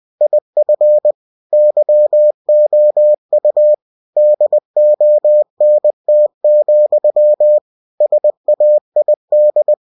Morse code, heard as a keyed tone, about 20 wpm, 600 Hz. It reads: IF YOU DONT, SAID